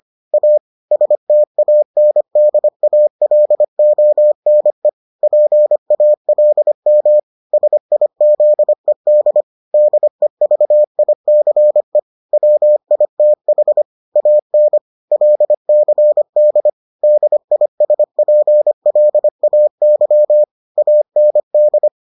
Morse code, heard as a keyed tone, about 25 wpm, 600 Hz. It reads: A STANDALONE PALM SIZED DEVICE WITH AN LCD DISPLAY AND